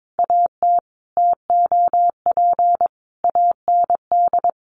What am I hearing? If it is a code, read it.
Morse code, heard as a keyed tone, about 22 wpm, 700 Hz. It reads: AT TOP AND
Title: Sentences from Top 1400 Words 22wpm